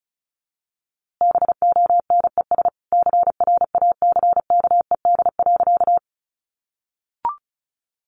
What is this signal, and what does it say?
Morse code, heard as a keyed tone, about 34 wpm, 700 Hz. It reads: BONES CRACKED.
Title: A Sound Of Thunder 35wpm